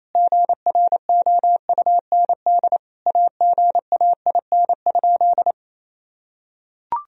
Morse code, heard as a keyed tone, about 28 wpm, 690 Hz. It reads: GROUND AGAIN?
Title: Sentences from Top 1000 Words - Review Part 2 - 28wpm